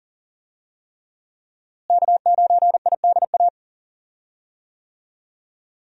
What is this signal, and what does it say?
Morse code, heard as a keyed tone, about 40 wpm, 700 Hz. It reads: K9IDA
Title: USA Call Signs - 5 Characters - Repeated 3x - 40wpm